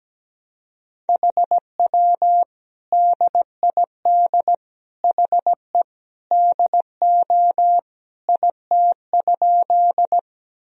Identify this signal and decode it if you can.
Morse code, heard as a keyed tone, about 17 wpm, 700 Hz. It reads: HW DID HE DO IT?
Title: Sentences from Top 100 Words and Abbreviations 17wpm